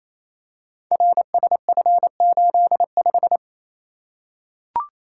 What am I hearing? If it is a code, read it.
Morse code, heard as a keyed tone, about 27 wpm, 700 Hz. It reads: RSF85
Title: Code Groups - 5 Letter:Numbers - 28wpm